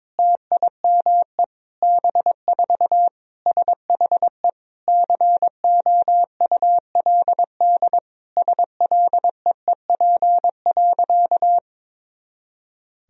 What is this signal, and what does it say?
Morse code, heard as a keyed tone, about 22 wpm, 700 Hz. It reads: TIME B4 SHE COULD SLEEP.
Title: Sentences from Top 500 Words and Abbreviations 22wpm